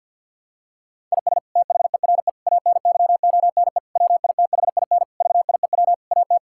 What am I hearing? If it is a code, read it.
Morse code, heard as a keyed tone, about 50 wpm, 700 Hz. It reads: IS THERE ANYONE WITHIN VIEW AT
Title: Sentences from Top 1000 Words - Review Part 6 - 50wpm